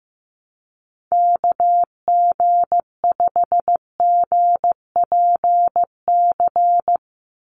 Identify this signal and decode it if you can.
Morse code, heard as a keyed tone, about 15 wpm, 700 Hz. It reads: KG5GPC
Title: USA Call Signs 15wpm